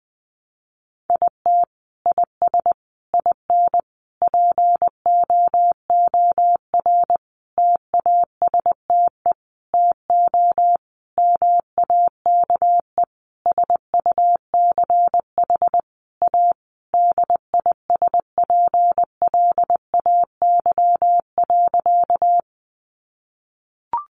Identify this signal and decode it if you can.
Morse code, heard as a keyed tone, about 20 wpm, 700 Hz. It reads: IT IS IN POOR TASTE TO MAKE SUCH A DISPLAY.
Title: Sentences from Top 1200 Words 20wpm